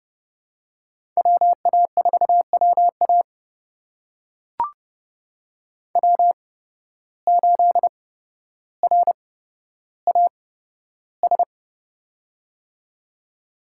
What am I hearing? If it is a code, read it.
Morse code, heard as a keyed tone, about 30 wpm, spaced as for 13 wpm, 700 Hz. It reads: WA4WA W8RAS
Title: USA Call Signs - 5 Characters - 8x Character Spacing - 30wpm